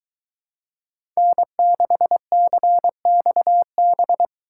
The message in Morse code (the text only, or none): N6CXB